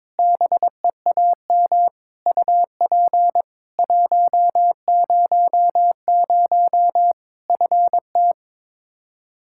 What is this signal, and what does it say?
Morse code, heard as a keyed tone, about 22 wpm, 700 Hz. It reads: BEAM UP 100 FT
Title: K1USN SST QSO Practice 22wpm